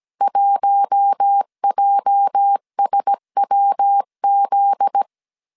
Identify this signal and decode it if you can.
Morse code, heard as a keyed tone, about 17 wpm, 800 Hz. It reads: 1JSWZ
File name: CW / DET / 1 9 1500C S A 16wpm 2828